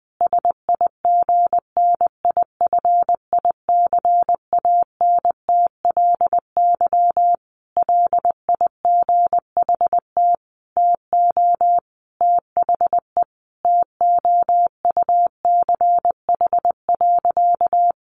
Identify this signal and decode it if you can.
Morse code, heard as a keyed tone, about 20 wpm, 700 Hz. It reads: SIGNIFICANTLY LIGHT TO THE TOUCH.
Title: Sentences from Top 1500 Words 20wpm